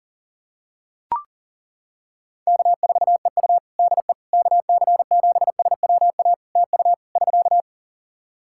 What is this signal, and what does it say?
Morse code, heard as a keyed tone, about 40 wpm, 700 Hz. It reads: K4EU DE KC7SWA TU <SK>
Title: Practice QSO - Exchange 4 - Ending - 40wpm